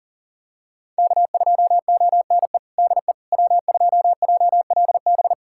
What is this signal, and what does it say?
Morse code, heard as a keyed tone, about 40 wpm, 700 Hz. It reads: K2ONE DE W2JLB
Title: USA Callsign DE Callsign - 5 Characters - 40wpm